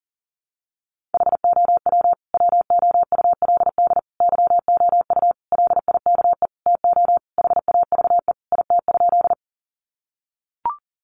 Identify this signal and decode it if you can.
Morse code, heard as a keyed tone, about 37 wpm, 700 Hz. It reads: HOW WOULD YOU LIKE TO HAVE IT?
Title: Sentences from Top 100 Words 40wpm